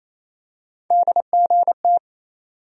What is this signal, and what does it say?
Morse code, heard as a keyed tone, about 28 wpm, 700 Hz. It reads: DGT
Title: USA Call Sign Postfixes - 3 Letters 28wpm